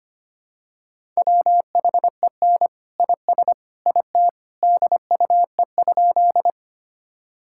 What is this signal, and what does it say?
Morse code, heard as a keyed tone, about 25 wpm, 700 Hz. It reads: WHEN IS IT DUE?
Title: Sentences from Top 1000 Words - Review Part 6 - 25wpm